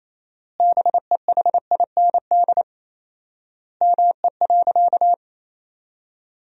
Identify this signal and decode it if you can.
Morse code, heard as a keyed tone, about 28 wpm, 700 Hz. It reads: BEHIND ME.
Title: 4x Word-Spacing - Sentences from Top 500 Words - 28wpm